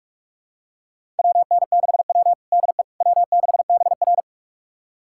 Morse code, heard as a keyed tone, about 46 wpm, 700 Hz. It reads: WN6W DE W6BR
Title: USA Callsign DE Callsign - 4 Characters - 45wpm